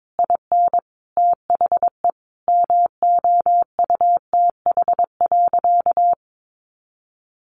Morse code, heard as a keyed tone, about 22 wpm, 700 Hz. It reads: IN THE MOUTH.
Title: Sentences from Top 1500 Words 22wpm